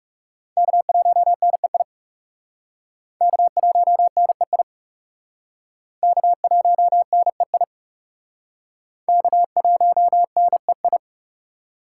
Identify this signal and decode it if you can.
Morse code, heard as a keyed tone, about 37 wpm, 700 Hz. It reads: K1NEI K1NEI K1NEI K1NEI
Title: Speed Racing - USA Call Signs 30wpm